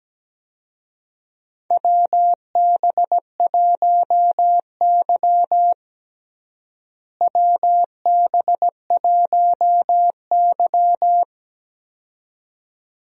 Morse code, heard as a keyed tone, about 17 wpm, 700 Hz. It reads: WB1Y WB1Y
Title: USA Call Signs - 4 Characters - Repeated 2x - 17wpm